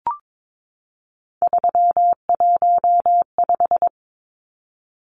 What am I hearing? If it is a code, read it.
Morse code, heard as a keyed tone, about 22 wpm, 700 Hz. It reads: 315
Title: Numbers - 3-Digit 22wpm